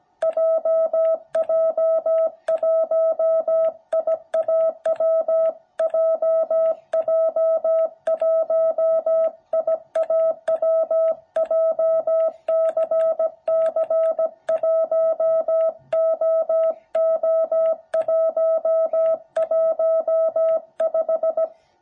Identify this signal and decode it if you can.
Morse code, heard as a keyed tone, about 17 wpm, 660 Hz. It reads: JJ1IAW JJ1IAW JCC1OO115
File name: jh8cbh / 2021cw